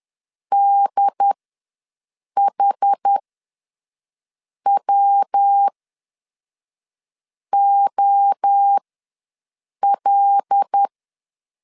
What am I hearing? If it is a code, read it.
Morse code, heard as a keyed tone, about 10 wpm, spaced as for 5 wpm, 800 Hz. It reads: DHWOL
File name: LEZIONI / 6 PKX0 / 6 PK ws 10wpm 3745